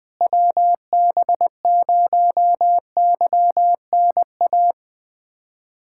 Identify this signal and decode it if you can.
Morse code, heard as a keyed tone, about 20 wpm, 700 Hz. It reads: WB0YNA
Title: USA Callsign DE Callsign - 6 Characters - 20wpm